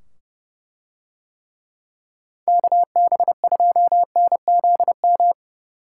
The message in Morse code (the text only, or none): KB2NZM